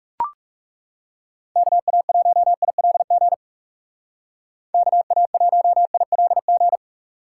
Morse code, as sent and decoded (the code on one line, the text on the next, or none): -.- .- .---- .. .-.. --. / -.- .- .---- .. .-.. --.
KA1ILG KA1ILG